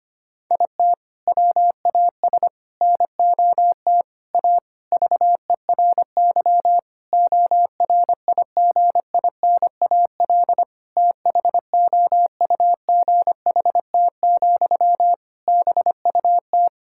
Morse code, heard as a keyed tone, about 25 wpm, 700 Hz. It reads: IT WAS NOT A VERY ORIGINAL THOUGHT, BUT
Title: Sentences from Top 1000 Words - Review Part 6 - 25wpm